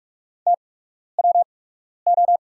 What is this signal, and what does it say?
Morse code, heard as a keyed tone, about 44 wpm, spaced as for 12 wpm, 690 Hz. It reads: TWO
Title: 8x Character Spacing - Top 500 Words - 45wpm